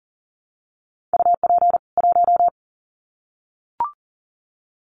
Morse code, heard as a keyed tone, about 40 wpm, 700 Hz. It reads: UP 1
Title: Common QSO Phrases 40wpm